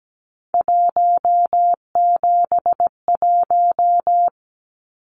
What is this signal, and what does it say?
Morse code, heard as a keyed tone, about 17 wpm, 700 Hz. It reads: 171
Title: Numbers - 3-Digit 17wpm